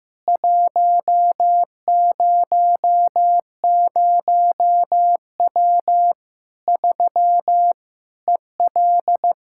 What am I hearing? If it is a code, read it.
Morse code, heard as a keyed tone, about 15 wpm, 700 Hz. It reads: 100W 3 EL
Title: K1USN SST QSO Practice 15wpm